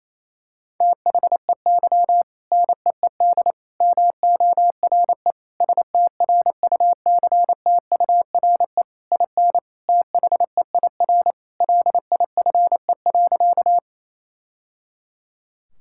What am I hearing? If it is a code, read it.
Morse code, heard as a keyed tone, about 28 wpm, 700 Hz. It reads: THEY NEED MORE STRUCTURE IN THEIR LIFE.
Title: Sentences from Top 1000 Words - Review Part 6 - 28wpm